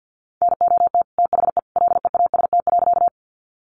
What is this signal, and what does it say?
Morse code, heard as a keyed tone, about 48 wpm, 700 Hz. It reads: NOT THE LEAST.
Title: Sentences from Top 700 Words 50wpm